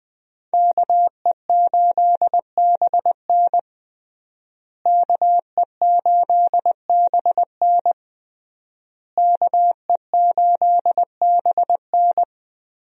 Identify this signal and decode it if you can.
Morse code, heard as a keyed tone, about 20 wpm, 700 Hz. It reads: KE8BN KE8BN KE8BN
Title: USA Call Signs - 5 Characters - Repeated 3x - 20wpm